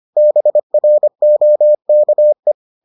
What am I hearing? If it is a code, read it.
Morse code, heard as a keyed tone, about 25 wpm, 600 Hz. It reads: BROKE